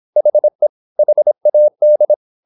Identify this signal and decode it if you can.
Morse code, heard as a keyed tone, about 26 wpm, 600 Hz. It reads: HE HAD
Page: Never 26wpm | CW med Gnister